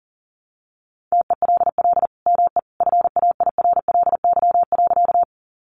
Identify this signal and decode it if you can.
Morse code, heard as a keyed tone, about 40 wpm, 700 Hz. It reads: TELL ME FAIRLY.
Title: Sentences from Top 1100 Words 40wpm